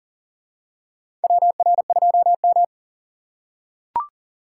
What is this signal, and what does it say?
Morse code, heard as a keyed tone, about 40 wpm, 700 Hz. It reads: WR2M
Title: USA Call Signs - 4 Characters - 2x Character Spacing - 40wpm